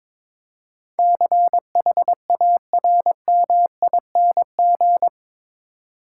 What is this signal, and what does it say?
Morse code, heard as a keyed tone, about 22 wpm, 700 Hz. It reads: CHARMING
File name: practice / 22wpm / Eight Letter Words - 22wpm